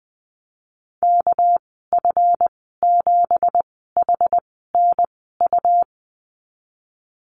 Morse code, heard as a keyed tone, about 20 wpm, spaced as for 15 wpm, 700 Hz. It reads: KF7HNU
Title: 219 Lesson 35g - Number 0 - Call Signs 20wpm